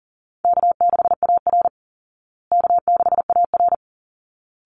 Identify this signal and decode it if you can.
Morse code, heard as a keyed tone, about 40 wpm, 700 Hz. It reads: K6AR K6AR
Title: Easy Call Signs Repeated 2x - 40wpm